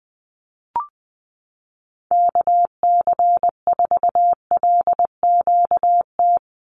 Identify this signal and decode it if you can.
Morse code, heard as a keyed tone, about 20 wpm, 700 Hz. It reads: KC4LQT
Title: USA Call Signs 20wpm